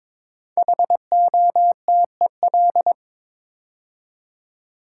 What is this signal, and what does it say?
Morse code, heard as a keyed tone, about 23 wpm, 700 Hz. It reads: HOTEL
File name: 5 Letter Words - First 3 Letters Given - 22wpm